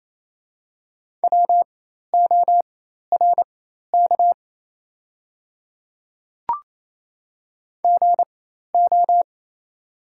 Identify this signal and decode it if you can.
Morse code, heard as a keyed tone, about 28 wpm, spaced as for 13 wpm, 700 Hz. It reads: WORK GO
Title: 4x Character Spacing - Top 200 Words - 28wpm